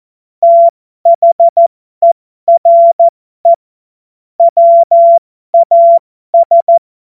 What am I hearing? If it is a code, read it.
Morse code, heard as a keyed tone, about 14 wpm, spaced as for 12 wpm, 680 Hz. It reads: THERE WAS